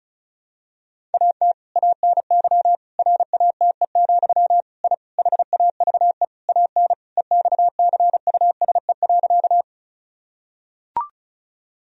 Morse code, heard as a keyed tone, about 34 wpm, 700 Hz. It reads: AT ANY RATE, I HAVE AN EXCUSE.
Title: Sentences from Top 1000 Words - Review Part 4 - 35wpm